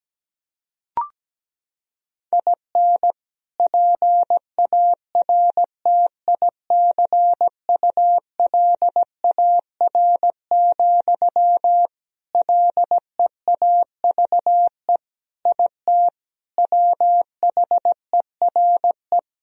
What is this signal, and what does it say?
Morse code, heard as a keyed tone, about 17 wpm, 700 Hz. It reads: IN PARTICULAR, LEAVE IT WHERE
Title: Sentences from Top 1000 Words - Review Part 4 - 17wpm